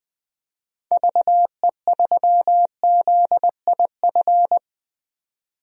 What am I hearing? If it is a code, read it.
Morse code, heard as a keyed tone, about 21 wpm, 700 Hz. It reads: VE3ZIF
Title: DXCC-100 Call Signs - 6 Characters - 20wpm